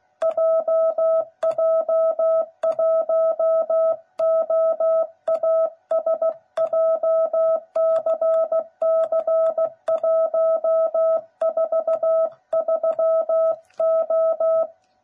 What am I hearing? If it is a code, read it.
Morse code, heard as a keyed tone, about 16 wpm, 660 Hz. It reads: JJ1OAS JCC143O
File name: jh8cbh / 2021cw / cw09b